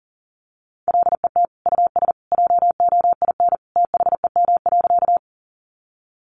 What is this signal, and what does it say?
Morse code, heard as a keyed tone, about 40 wpm, 700 Hz. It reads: LET US JOIN THEM.
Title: Sentences from Top 1400 Words 40wpm